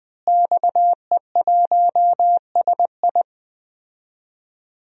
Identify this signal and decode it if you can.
Morse code, heard as a keyed tone, about 20 wpm, 700 Hz. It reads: XE1SI
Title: DXCC-100 Call Signs - 5 Characters - Repeated 3x - 20wpm